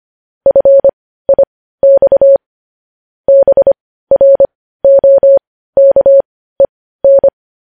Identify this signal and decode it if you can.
Morse code, heard as a keyed tone, about 25 wpm, spaced as for 15 wpm, 550 Hz. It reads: FIX BROKEN
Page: news-headlines-in-morse-code-at-15-wpm